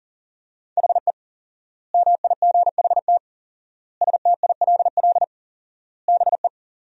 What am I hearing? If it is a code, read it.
Morse code, heard as a keyed tone, about 37 wpm, 700 Hz. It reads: HE MIGHT STILL BE
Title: 4x Word-Spacing - Sentences from Top 300 Words - 40wpm